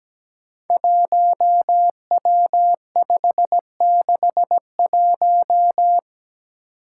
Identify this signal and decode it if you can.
Morse code, heard as a keyed tone, about 17 wpm, 700 Hz. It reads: 1W561
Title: Five Challenging Letter-Number Combinations 17wpm